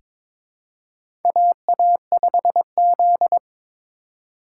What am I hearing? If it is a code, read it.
Morse code, heard as a keyed tone, about 22 wpm, 700 Hz. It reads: AA5Z
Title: USA Call Signs - 4 Characters - 2x Character Spacing - 22wpm